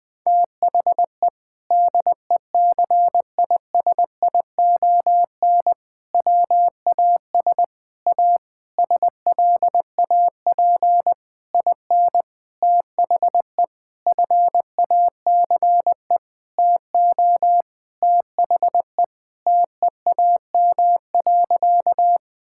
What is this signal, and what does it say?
Morse code, heard as a keyed tone, about 20 wpm, 700 Hz. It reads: THE DECISION WAS A SLAP IN THE FACE TO THE TEAM.
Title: Sentences with 6-Word Phrases - 20wpm